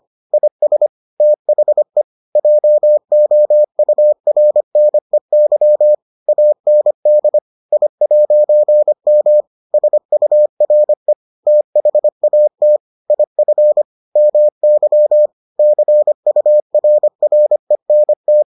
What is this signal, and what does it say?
Morse code, heard as a keyed tone, about 25 wpm, 600 Hz. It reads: IS THE JOURNEY AND I'M SURE THAT IF MY CURRENT